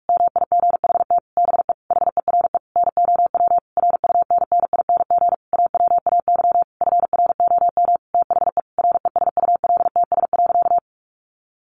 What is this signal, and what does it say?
Morse code, heard as a keyed tone, about 44 wpm, 700 Hz. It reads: MIGHT BE HERE NOW RUNNING AWAY FROM THE RESULTS.
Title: A Sound Of Thunder 45wpm